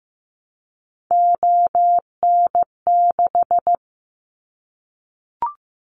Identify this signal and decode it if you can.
Morse code, heard as a keyed tone, about 15 wpm, 700 Hz. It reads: ON6
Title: Callsign Prefixes for Top 100 DXCC - 15wpm